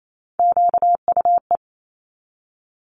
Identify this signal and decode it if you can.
Morse code, heard as a keyed tone, about 28 wpm, 700 Hz. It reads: QUE
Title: Multi-letter Phonemes 28wpm